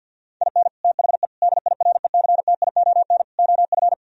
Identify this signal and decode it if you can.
Morse code, heard as a keyed tone, about 64 wpm, 700 Hz. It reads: IN THE DIRECTION OF